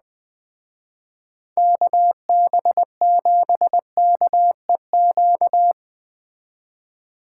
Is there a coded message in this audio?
Yes: Morse code from 1.6 to 5.7 s, heard as a keyed tone, about 20 wpm, 700 Hz. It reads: KB7KEQ